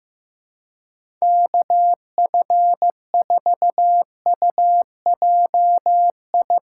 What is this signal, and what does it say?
Morse code, heard as a keyed tone, about 16 wpm, 700 Hz. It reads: KF4UJI
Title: Speed Racing - USA 2x3 Call Signs 15wpm